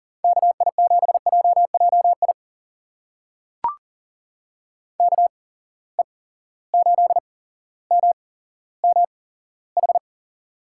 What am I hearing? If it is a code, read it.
Morse code, heard as a keyed tone, about 38 wpm, spaced as for 17 wpm, 700 Hz. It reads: KI7JJI KE8MMH
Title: USA Call Signs - 6 Characters - 8x Character Spacing - 40wpm